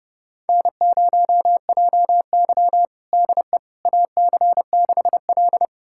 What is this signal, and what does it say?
Morse code, heard as a keyed tone, about 30 wpm, 700 Hz. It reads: N0JY DE AC6L
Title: USA Callsign DE Callsign - 4 Characters - 30wpm